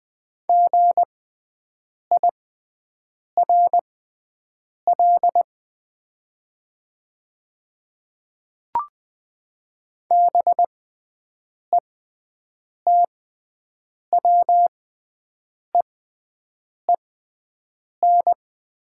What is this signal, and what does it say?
Morse code, heard as a keyed tone, about 20 wpm, spaced as for 7 wpm, 690 Hz. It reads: GIRL BETWEEN
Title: 6x Character Spacing - Top 500 Words - 20wpm